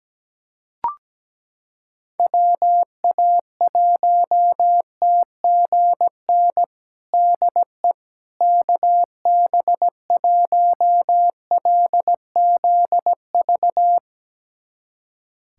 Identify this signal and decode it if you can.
Morse code, heard as a keyed tone, about 17 wpm, 700 Hz. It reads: WA1TGN DE KB1LZV
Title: USA Callsign DE Callsign - 6 Characters - 17wpm